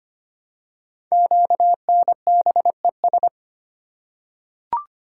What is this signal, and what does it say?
Morse code, heard as a keyed tone, about 25 wpm, 700 Hz. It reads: QNBES
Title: Code Groups - 5 Letters - 25wpm